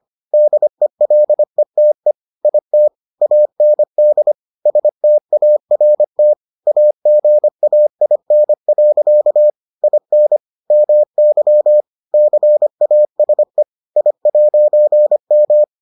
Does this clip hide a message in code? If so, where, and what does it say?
Yes: Morse code from 0.3 to 15.6 s, heard as a keyed tone, about 25 wpm, 600 Hz. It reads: DELETE IT AND START AGAIN. IN MY CASE I'M